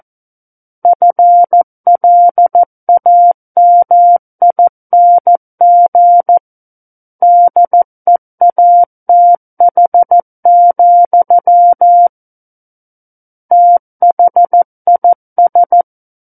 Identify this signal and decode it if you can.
Morse code, heard as a keyed tone, about 14 wpm, 700 Hz. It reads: FLAMING DEATH, THIS
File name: Book One / 05-Chapter 5 - 14 WPM